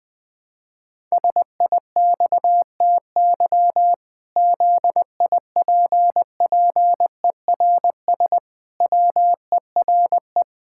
Morse code, heard as a keyed tone, about 20 wpm, 700 Hz. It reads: SIXTY ZIPPERS WERE
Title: Pangram Challenge v2 - 20wpm